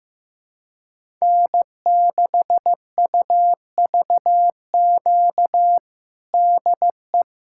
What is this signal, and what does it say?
Morse code, heard as a keyed tone, about 15 wpm, 700 Hz. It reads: N6UVQ DE
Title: USA Callsign DE Callsign - 5 Characters - 15wpm